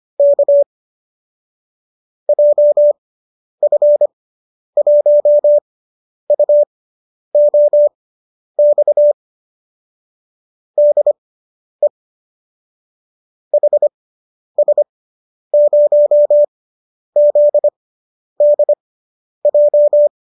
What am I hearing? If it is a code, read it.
Morse code, heard as a keyed tone, about 25 wpm, spaced as for 10 wpm, 590 Hz. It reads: K JF1UOX DE HS0ZDJ